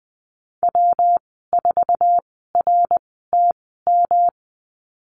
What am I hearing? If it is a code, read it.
Morse code, heard as a keyed tone, about 20 wpm, spaced as for 14 wpm, 700 Hz. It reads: W4RTM
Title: 154 Lesson 25g - Letter p - Call Signs 20wpm